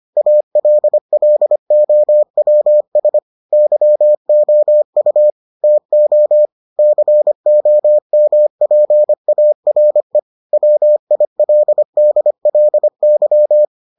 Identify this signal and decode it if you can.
Morse code, heard as a keyed tone, about 25 wpm, 600 Hz. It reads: ALLOWS YOU TO COMPARE WILDLY